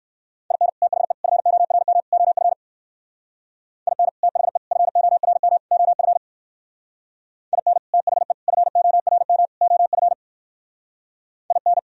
Morse code, heard as a keyed tone, about 56 wpm, 700 Hz. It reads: IN THE FORM OF IN THE FORM OF IN THE FORM OF IN
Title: Speed-Racing - Sets of 4 Words 45wpm